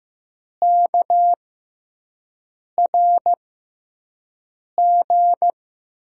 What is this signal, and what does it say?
Morse code, heard as a keyed tone, about 15 wpm, spaced as for 5 wpm, 700 Hz. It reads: KRG
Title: USA Call Signs - 5 Characters - 6x Character Spacing - 15wpm